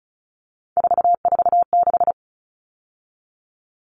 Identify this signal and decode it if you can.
Morse code, heard as a keyed tone, about 34 wpm, 700 Hz. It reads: 446
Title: Numbers - 3-Digit 35wpm